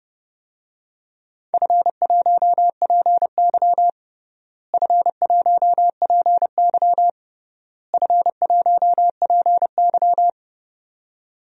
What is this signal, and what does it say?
Morse code, heard as a keyed tone, about 32 wpm, 700 Hz. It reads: F1PY F1PY F1PY
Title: DXCC-100 Call Signs - 4 Characters - Repeated 3x - 30wpm